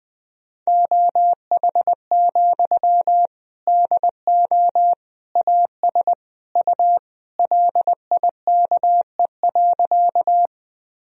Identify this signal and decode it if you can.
Morse code, heard as a keyed tone, about 21 wpm, 700 Hz. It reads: OH, DO AS U LIKE.
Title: Sentences from Top 500 Words and Abbreviations 20wpm